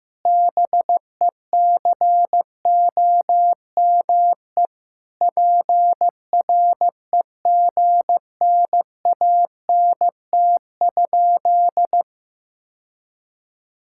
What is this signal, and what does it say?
Morse code, heard as a keyed tone, about 15 wpm, 700 Hz. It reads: BECOME PREGNANT?
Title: Sentences from Top 1000 Words - Review Part 6 - 15wpm